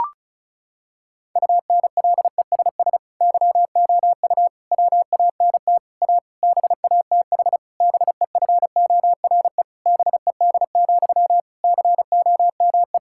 Morse code, heard as a keyed tone, about 36 wpm, 700 Hz. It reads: UNLESS YOU WANT A BATH BEFORE BED, COME
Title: Sentences from Top 1600 Words 35wpm